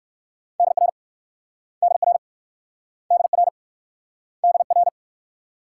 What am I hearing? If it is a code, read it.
Morse code, heard as a keyed tone, about 59 wpm, 700 Hz. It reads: DR DR DR DR